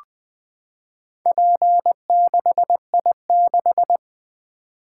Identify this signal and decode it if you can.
Morse code, heard as a keyed tone, about 20 wpm, 700 Hz. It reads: P6I6
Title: Four Letter-Number Combinations - 20wpm